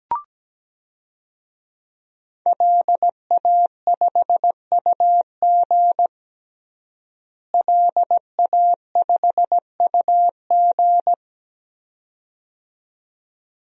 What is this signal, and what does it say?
Morse code, heard as a keyed tone, about 17 wpm, 700 Hz. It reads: LA5UG LA5UG
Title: DXCC-100 Call Signs - 5 Characters - Repeated 2x - 17wpm